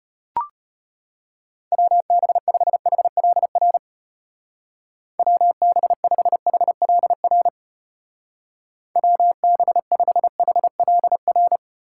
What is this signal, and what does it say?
Morse code, heard as a keyed tone, about 33 wpm, 700 Hz. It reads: WB5HLR WB5HLR WB5HLR